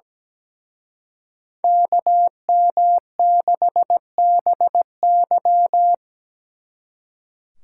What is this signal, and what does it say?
Morse code, heard as a keyed tone, about 20 wpm, 700 Hz. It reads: KM6BY
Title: Speed Racing - USA 2x2 Call Signs 17wpm